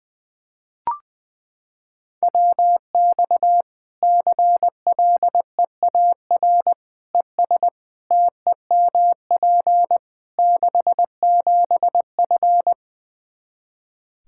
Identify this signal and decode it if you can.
Morse code, heard as a keyed tone, about 21 wpm, 700 Hz. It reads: WX CLEAR ES TEMP 67F
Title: K1USN SST QSO Practice 20wpm